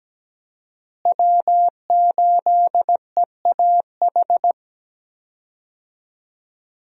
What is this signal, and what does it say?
Morse code, heard as a keyed tone, about 17 wpm, 700 Hz. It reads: W8EAH
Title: USA Call Signs - 5 Characters - Repeated 2x - 17wpm